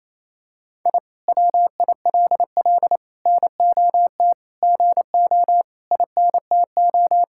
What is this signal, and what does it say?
Morse code, heard as a keyed tone, about 28 wpm, 700 Hz. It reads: I WILL NOT GO INTO
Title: Sentences from Top 1000 Words - Review Part 6 - 28wpm